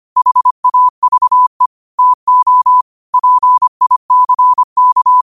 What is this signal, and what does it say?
Morse code, heard as a keyed tone, about 25 wpm, 1000 Hz. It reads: HAVE TO PICK
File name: 25wpm